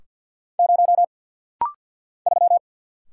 Morse code, heard as a keyed tone, about 50 wpm, 700 Hz. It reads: 0 3
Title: Single Letter-Number - Rapid-Fire 50wpm